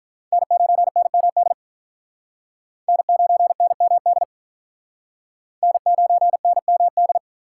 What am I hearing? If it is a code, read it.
Morse code, heard as a keyed tone, about 45 wpm, 700 Hz. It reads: N9NMD N9NMD N9NMD